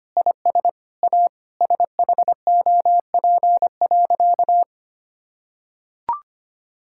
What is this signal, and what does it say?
Morse code, heard as a keyed tone, about 25 wpm, 700 Hz. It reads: IS A SHOP.
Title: Sentences from Top 1600 Words 25wpm